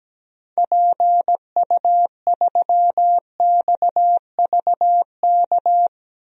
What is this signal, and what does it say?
Morse code, heard as a keyed tone, about 17 wpm, 700 Hz. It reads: PU3XVK
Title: DXCC-100 Call Signs - 6 Characters - 17wpm